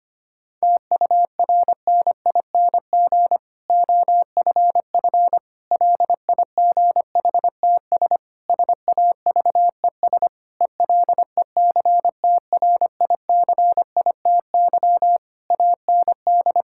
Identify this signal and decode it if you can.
Morse code, heard as a keyed tone, about 25 wpm, 700 Hz. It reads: TURNING OFF LIGHTS SAVES ELECTRICITY AND
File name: practice / 25wpm / Difficult Sentences with Hints - 25wpm